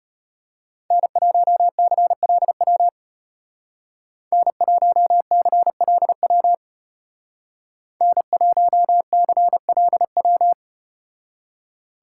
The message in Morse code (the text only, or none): N1CLW N1CLW N1CLW